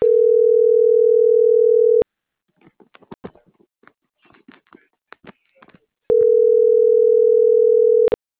Ringback tone (North America), 440 + 480 Hz.